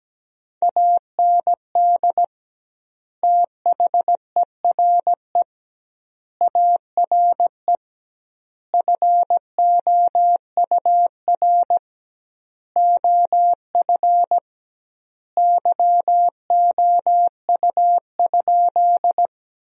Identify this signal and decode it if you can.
Morse code, heard as a keyed tone, about 17 wpm, 700 Hz. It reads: AND THERE ARE FOUR OF YOU?
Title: 2x Word-Spacing - Sentences from Top 300 Words - 17wpm